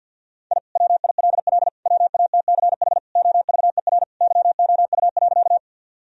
Morse code, heard as a keyed tone, about 45 wpm, 700 Hz. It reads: I WILL WATCH OVER YOU.
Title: Sentences from Top 1000 Words - Review Part 1 - 50wpm